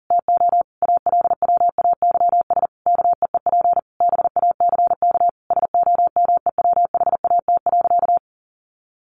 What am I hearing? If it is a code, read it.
Morse code, heard as a keyed tone, about 37 wpm, 700 Hz. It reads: TO ALWAYS KEEP BACK SOMEWHAT.
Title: Sentences from Top 1300 Words 40wpm